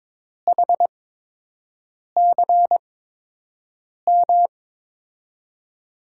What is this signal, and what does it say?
Morse code, heard as a keyed tone, about 22 wpm, spaced as for 6 wpm, 700 Hz. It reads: HCM
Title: USA Call Signs - 5 Characters - 8x Character Spacing - 22wpm